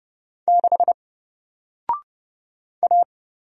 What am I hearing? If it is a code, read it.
Morse code, heard as a keyed tone, about 30 wpm, 700 Hz. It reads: 6 A
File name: Single-Letter-Number-30wpm